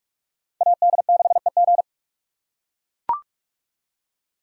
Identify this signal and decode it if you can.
Morse code, heard as a keyed tone, about 44 wpm, 700 Hz. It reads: AD6EG